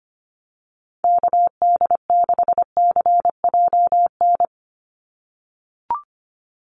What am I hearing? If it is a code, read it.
Morse code, heard as a keyed tone, about 25 wpm, 690 Hz. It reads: KD6CJN
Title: USA Call Signs 25wpm